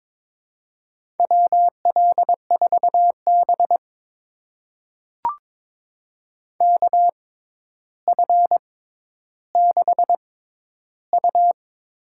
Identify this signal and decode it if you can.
Morse code, heard as a keyed tone, about 22 wpm, spaced as for 11 wpm, 700 Hz. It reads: WL4B KF6U
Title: USA Call Signs - 6x Character Spacing - 22wpm